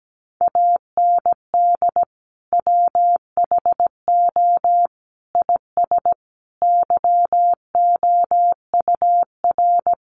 Morse code, heard as a keyed tone, about 17 wpm, 700 Hz. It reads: AND WHO IS YOUR
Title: Sentences from Top 1300 Words 17wpm